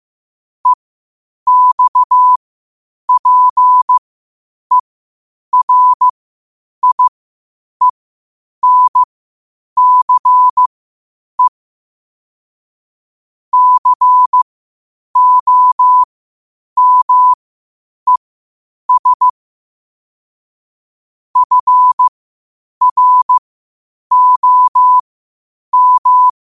Morse code, heard as a keyed tone, about 15 wpm, spaced as for 8 wpm, 1000 Hz. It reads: EXPERIENCE COMES FROM
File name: qotdfiles / 8wpm